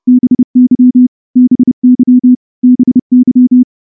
Morse code, heard as a keyed tone, about 29 wpm, 250 Hz. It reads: BY BY BY
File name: morse-training-30WPM